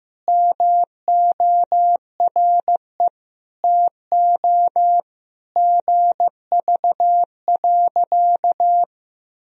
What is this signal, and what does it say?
Morse code, heard as a keyed tone, about 15 wpm, 700 Hz. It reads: MORE TO GV.
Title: Sentences from Top 200 Words and Abbreviations 15wpm